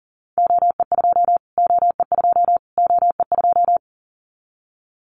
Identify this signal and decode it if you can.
Morse code, heard as a keyed tone, about 40 wpm, 700 Hz. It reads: OE2 OE2 OE2
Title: Callsign Prefixes for Top 25 DXCC - Repeated 3x - 40wpm